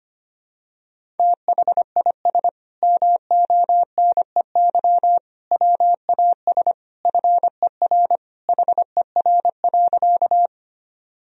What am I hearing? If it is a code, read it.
Morse code, heard as a keyed tone, about 25 wpm, 700 Hz. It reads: THIS MONEY WAS FER HER.
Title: Sentences from Top 400 Words and Abbreviations 25wpm